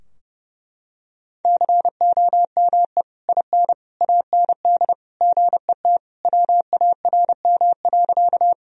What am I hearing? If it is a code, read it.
Morse code, heard as a keyed tone, about 30 wpm, 700 Hz. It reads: COME IN AND GET WARM.
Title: Sentences from Top 1000 Words - Review Part 2 - 30wpm